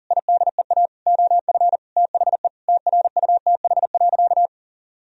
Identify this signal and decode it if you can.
Morse code, heard as a keyed tone, about 40 wpm, 700 Hz. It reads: IDEA OF THE TRUTH.
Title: Sentences from Top 1000 Words - Review Part 1 - 40wpm